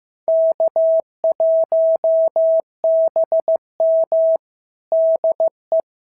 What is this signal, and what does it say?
Morse code, heard as a keyed tone, about 15 wpm, 650 Hz. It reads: K1BM DE
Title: SOTA QSO Practice 15wpm